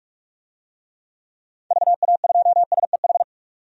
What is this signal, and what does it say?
Morse code, heard as a keyed tone, about 42 wpm, 700 Hz. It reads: VA2SEH
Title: DXCC-100 Call Signs - 6 Characters - 45wpm